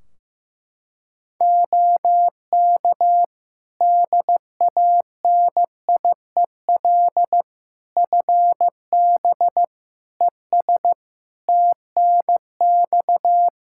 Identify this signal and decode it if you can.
Morse code, heard as a keyed tone, about 15 wpm, 700 Hz. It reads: OK DANIEL FB ES TNX
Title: K1USN SST QSO Practice 15wpm